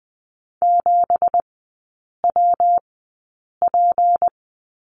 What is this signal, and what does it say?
Morse code, heard as a keyed tone, about 20 wpm, spaced as for 8 wpm, 700 Hz. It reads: 7WP
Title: 172 Lesson 28e - Number 9 - Three Characters 20wpm